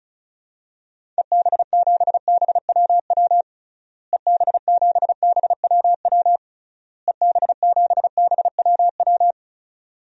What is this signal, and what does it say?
Morse code, heard as a keyed tone, about 36 wpm, 700 Hz. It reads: EB7BWW EB7BWW EB7BWW
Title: DXCC-100 Call Signs - 6 Characters - Repeated 3x - 35wpm